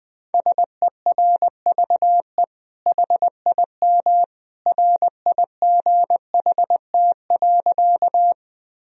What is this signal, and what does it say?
Morse code, heard as a keyed tone, about 20 wpm, 700 Hz. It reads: SERVE HIM RIGHT.
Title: Sentences from Top 1000 Words - Review Part 2 - 20wpm